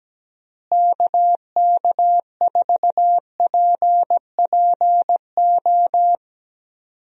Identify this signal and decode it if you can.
Morse code, heard as a keyed tone, about 17 wpm, 700 Hz. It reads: KK4PPO
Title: USA Call Signs - 6 Characters - Repeated 2x - 17wpm